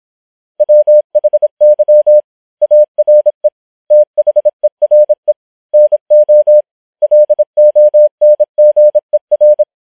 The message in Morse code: WHY ARE THERE NO LONGER